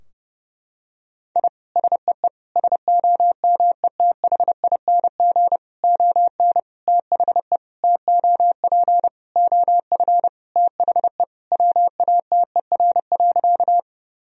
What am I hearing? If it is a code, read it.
Morse code, heard as a keyed tone, about 32 wpm, 700 Hz. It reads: I SEE SOMETHING ON THE TOP OF THE WATER.
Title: Sentences from Top 1000 Words - Review Part 2 - 30wpm